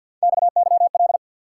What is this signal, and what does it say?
Morse code, heard as a keyed tone, about 48 wpm, 700 Hz. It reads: XYL
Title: Top 90 QSO Elements 50wpm